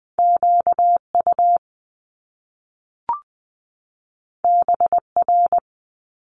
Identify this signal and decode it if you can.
Morse code, heard as a keyed tone, about 20 wpm, 700 Hz. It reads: Q U B R
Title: Top Two Letter Combinations in Words 20wpm